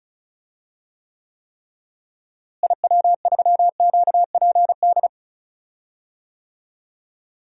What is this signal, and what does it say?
Morse code, heard as a keyed tone, about 36 wpm, 700 Hz. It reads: IW3QPD
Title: DXCC-100 Call Signs - 6 Characters - Repeated 3x - 35wpm